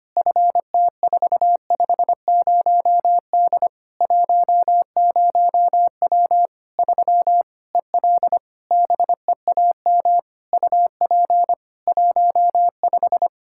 Morse code, heard as a keyed tone, about 25 wpm, 700 Hz. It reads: FT450D 10W 3 EL BEAM UP 15
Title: Practice QSO - Exchange 2 - Info - 25wpm